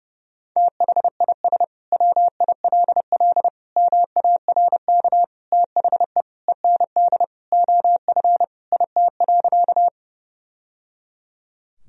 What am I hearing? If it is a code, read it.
Morse code, heard as a keyed tone, about 30 wpm, 700 Hz. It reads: THIS WILL MARK THE END OF IT.
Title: Sentences from Top 1000 Words - Review Part 1 - 30wpm